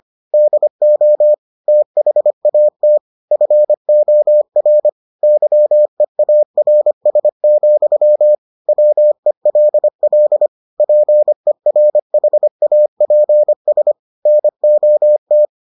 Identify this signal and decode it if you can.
Morse code, heard as a keyed tone, about 25 wpm, 600 Hz. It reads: DO THAT FOR YEARS, WELL PERHAPS NOT